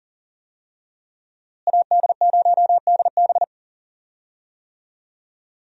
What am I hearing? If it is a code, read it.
Morse code, heard as a keyed tone, about 40 wpm, 700 Hz. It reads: AD0DB
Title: USA Call Signs - 5 Characters - Repeated 2x - 40wpm